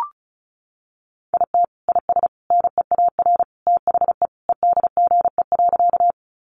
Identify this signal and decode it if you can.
Morse code, heard as a keyed tone, about 34 wpm, 700 Hz. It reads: IT IS NEAR THE EDGE.
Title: Additional Easy Phrases 35wpm